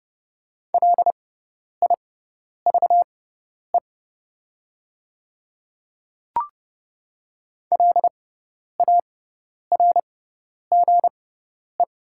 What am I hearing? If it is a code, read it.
Morse code, heard as a keyed tone, about 30 wpm, spaced as for 10 wpm, 700 Hz. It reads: LIVE LARGE
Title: 6x Character Spacing - Top 300 Words - 30wpm